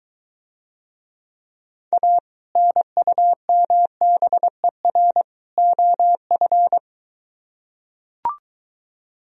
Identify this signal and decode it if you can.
Morse code, heard as a keyed tone, about 23 wpm, 700 Hz. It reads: A NUMBER OF